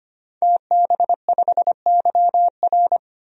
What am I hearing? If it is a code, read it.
Morse code, heard as a keyed tone, about 25 wpm, 700 Hz. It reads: TB5YR
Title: DXCC-100 Call Signs - 5 Characters - 25wpm